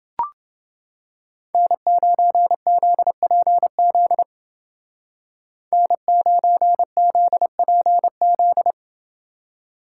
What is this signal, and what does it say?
Morse code, heard as a keyed tone, about 29 wpm, 700 Hz. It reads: N9ZPZ N9ZPZ